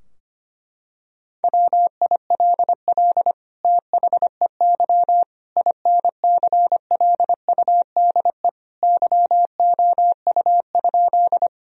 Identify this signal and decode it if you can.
Morse code, heard as a keyed tone, about 25 wpm, 700 Hz. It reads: WILL THEY INCLUDE YOU?
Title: Sentences from Top 1000 Words - Review Part 3 - 25wpm